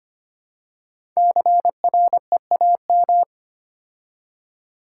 Morse code, heard as a keyed tone, about 25 wpm, 700 Hz. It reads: CREAM
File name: 5 Letter Words - First 3 Letters Given - 25wpm